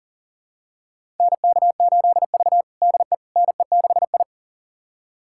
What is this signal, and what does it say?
Morse code, heard as a keyed tone, about 39 wpm, 700 Hz. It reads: NK8V DE NE6I
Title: USA Callsign DE Callsign - 4 Characters - 40wpm